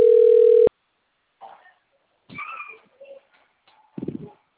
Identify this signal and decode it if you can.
Ringback tone (North America), 440 + 480 Hz.